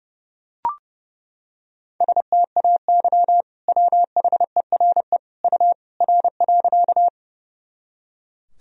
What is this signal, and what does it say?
Morse code, heard as a keyed tone, about 30 wpm, 700 Hz. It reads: STAY WHERE U R.
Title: Sentences from Top 500 Words and Abbreviations 30wpm